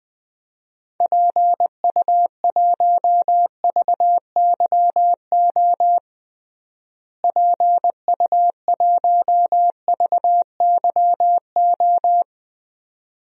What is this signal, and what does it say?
Morse code, heard as a keyed tone, about 20 wpm, 700 Hz. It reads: PU1VYO PU1VYO
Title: DXCC-100 Call Signs - 6 Characters - Repeated 2x - 20wpm